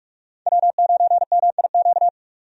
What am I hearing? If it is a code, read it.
Morse code, heard as a keyed tone, about 46 wpm, 700 Hz. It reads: W9MIQ
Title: USA Call Signs - 5 Characters - 45wpm